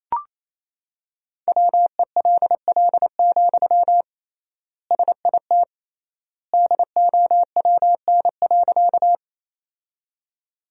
Morse code, heard as a keyed tone, about 28 wpm, 700 Hz. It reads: WELL, SIT DOWN.
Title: 3x Word-Spacing - Sentences from Top 500 Words - 28wpm